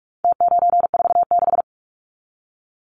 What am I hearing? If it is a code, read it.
Morse code, heard as a keyed tone, about 44 wpm, 700 Hz. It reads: T946
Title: Numbers - Contest Style - 4-Digit 45wpm